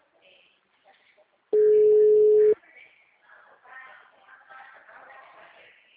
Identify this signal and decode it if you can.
Ringback tone (Europe, 425 Hz), with speech.